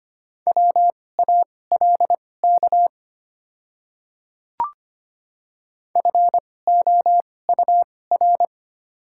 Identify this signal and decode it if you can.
Morse code, heard as a keyed tone, about 25 wpm, spaced as for 18 wpm, 700 Hz. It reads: WALK FOUR
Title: 2x Character Spacing - Top 300 Words - 25wpm